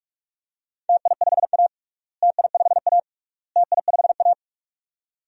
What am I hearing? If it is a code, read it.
Morse code, heard as a keyed tone, about 41 wpm, 700 Hz. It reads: TI5A TI5A TI5A
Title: DXCC-100 Call Signs - 4 Characters - Repeated 3x - 45wpm